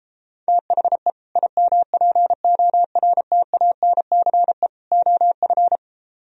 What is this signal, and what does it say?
Morse code, heard as a keyed tone, about 33 wpm, 700 Hz. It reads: THE IMPORTANCE OF